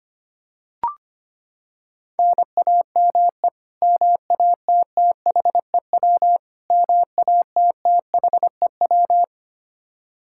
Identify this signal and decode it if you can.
Morse code, heard as a keyed tone, about 25 wpm, 700 Hz. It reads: NAME MATTHEW MATTHEW
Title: K1USN SST QSO Practice 25wpm